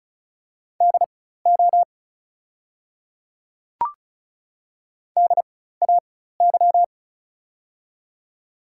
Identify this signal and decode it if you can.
Morse code, heard as a keyed tone, about 34 wpm, spaced as for 16 wpm, 690 Hz. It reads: DO DAY
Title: 4x Character Spacing - Top 200 Words - 35wpm